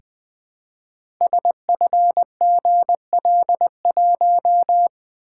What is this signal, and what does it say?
Morse code, heard as a keyed tone, about 20 wpm, 700 Hz. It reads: SFGL1
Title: Code Groups - 5 Letter:Numbers - 20wpm